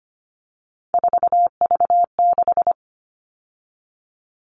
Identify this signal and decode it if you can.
Morse code, heard as a keyed tone, about 25 wpm, 700 Hz. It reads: 4V6
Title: All Cut Numbers - 3-Digit 25wpm